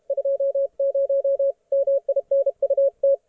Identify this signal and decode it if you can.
Morse code, heard as a keyed tone, about 33 wpm, 560 Hz. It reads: 20 MINUT